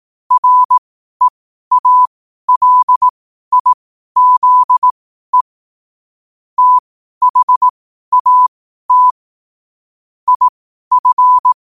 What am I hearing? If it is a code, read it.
Morse code, heard as a keyed tone, about 18 wpm, spaced as for 12 wpm, 1000 Hz. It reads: REALIZE THAT IF
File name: qotdfiles / 12wpm